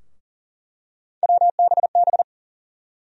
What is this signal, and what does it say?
Morse code, heard as a keyed tone, about 40 wpm, 700 Hz. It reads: WBB
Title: USA Call Sign Postfixes - 3 Letters 40wpm